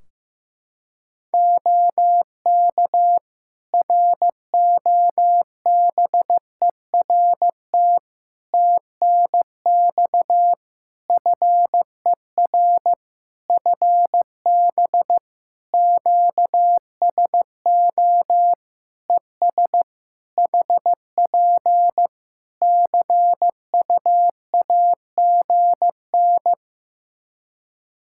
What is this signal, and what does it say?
Morse code, heard as a keyed tone, about 15 wpm, 700 Hz. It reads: OK ROBERT TNX FER FB QSO ES HP CUAGN
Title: K1USN SST QSO Practice 15wpm